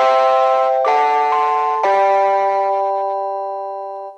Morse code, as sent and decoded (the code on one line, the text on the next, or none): none